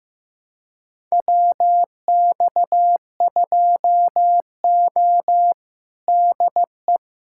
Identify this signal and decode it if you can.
Morse code, heard as a keyed tone, about 15 wpm, 700 Hz. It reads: WX2O DE
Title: USA Callsign DE Callsign - 4 Characters - 15wpm